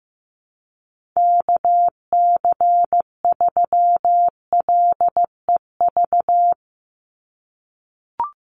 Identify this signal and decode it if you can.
Morse code, heard as a keyed tone, about 15 wpm, 700 Hz. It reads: KC3LEV
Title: USA Call Signs 15wpm